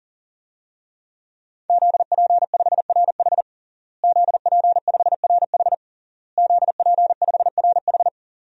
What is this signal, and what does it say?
Morse code, heard as a keyed tone, about 40 wpm, 700 Hz. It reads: ZP5RH ZP5RH ZP5RH
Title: DXCC-100 Call Signs - 5 Characters - Repeated 3x - 40wpm